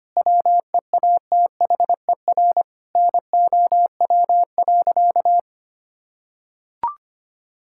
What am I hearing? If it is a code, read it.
Morse code, heard as a keyed tone, about 25 wpm, 700 Hz. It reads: WEATHER NOW.
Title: Sentences from Top 1000 Words - Review Part 5 - 25wpm